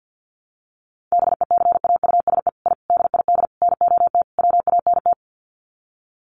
Morse code, heard as a keyed tone, about 50 wpm, 700 Hz. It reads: BECAUSE I DID NOT WANT
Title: Sets of 5 Words Encore2 50wpm